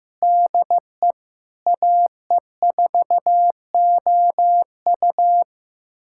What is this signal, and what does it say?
Morse code, heard as a keyed tone, about 15 wpm, 700 Hz. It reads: DE AE4OU
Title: USA Callsign DE Callsign - 5 Characters - 15wpm